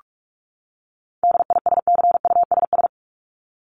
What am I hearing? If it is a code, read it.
Morse code, heard as a keyed tone, about 44 wpm, 700 Hz. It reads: DISCUSS
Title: ED Suffix - Top 100 Words with Base Words 45wpm